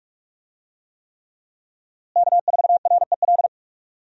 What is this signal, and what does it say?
Morse code, heard as a keyed tone, about 46 wpm, 700 Hz. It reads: K4REL
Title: USA Call Signs - 5 Characters - Repeated 3x - 45wpm